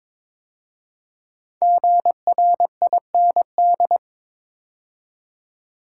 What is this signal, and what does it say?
Morse code, heard as a keyed tone, about 22 wpm, 700 Hz. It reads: GRIND
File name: 5 Letter Words - First 3 Letters Given - 22wpm